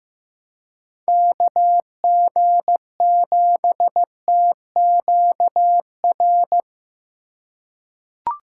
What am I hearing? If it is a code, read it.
Morse code, heard as a keyed tone, about 15 wpm, 700 Hz. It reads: KG7TQR
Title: USA Call Signs - 3x Character Spacing - 15wpm